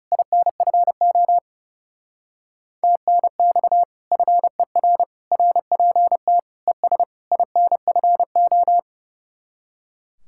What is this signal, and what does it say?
Morse code, heard as a keyed tone, about 38 wpm, 700 Hz. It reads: INFO TNX FER RPT ES INFO